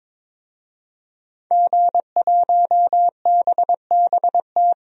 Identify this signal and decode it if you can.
Morse code, heard as a keyed tone, about 22 wpm, 700 Hz. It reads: G1BBT
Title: DXCC-100 Call Signs - 5 Characters - 22wpm